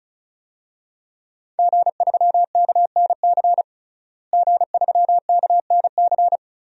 Morse code, heard as a keyed tone, about 34 wpm, 700 Hz. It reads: G3KNC G3KNC
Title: DXCC-100 Call Signs - 5 Characters - Repeated 3x - 35wpm